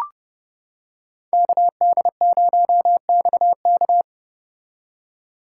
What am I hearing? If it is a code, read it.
Morse code, heard as a keyed tone, about 29 wpm, 700 Hz. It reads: KD0XK